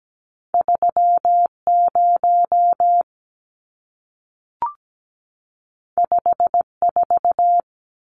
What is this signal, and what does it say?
Morse code, heard as a keyed tone, about 17 wpm, 700 Hz. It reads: 30 54
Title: Numbers - 2-Digit 17wpm